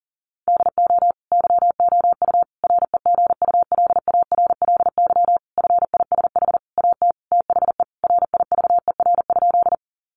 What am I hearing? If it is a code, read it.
Morse code, heard as a keyed tone, about 40 wpm, 700 Hz. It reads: DO YOU REGULARLY FISH AT THE RIVER?
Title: Sentences from Top 1400 Words 40wpm